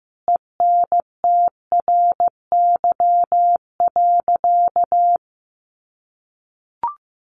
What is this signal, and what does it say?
Morse code, heard as a keyed tone, about 15 wpm, 700 Hz. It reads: ENTRY.
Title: Sentences from Top 1500 Words 15wpm